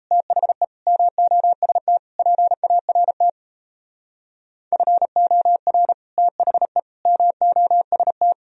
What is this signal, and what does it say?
Morse code, heard as a keyed tone, about 35 wpm, 700 Hz. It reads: THE MOST PART FOR THE MOST
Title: Speed-Racing - Sets of 4 Words 28wpm